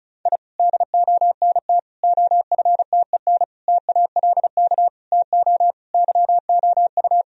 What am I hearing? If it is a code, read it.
Morse code, heard as a keyed tone, about 34 wpm, 700 Hz. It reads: I DONT OFTEN TALK TO YOU
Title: Sentences from Top 1000 Words - Review Part 1 - 35wpm